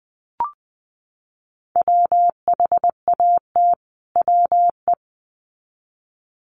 Morse code, heard as a keyed tone, about 20 wpm, 700 Hz. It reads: WHAT WE
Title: Sets of 2 Words 20wpm